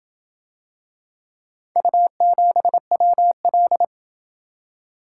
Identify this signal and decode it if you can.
Morse code, heard as a keyed tone, about 26 wpm, 700 Hz. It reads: U7WL